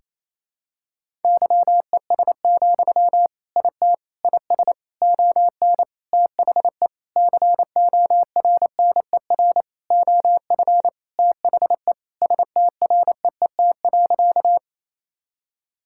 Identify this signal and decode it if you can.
Morse code, heard as a keyed tone, about 28 wpm, 700 Hz. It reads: YES, IT IS ON THE CORNER OF THE STREET.
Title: Sentences from Top 1600 Words 28wpm